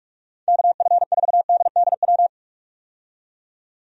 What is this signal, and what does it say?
Morse code, heard as a keyed tone, about 46 wpm, 700 Hz. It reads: KF4DDW
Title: USA Call Signs - 6 Characters - 45wpm